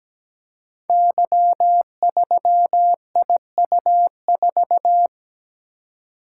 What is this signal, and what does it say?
Morse code, heard as a keyed tone, about 17 wpm, 700 Hz. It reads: Y3IU4
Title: Code Groups - 5 Letter:Numbers - 17wpm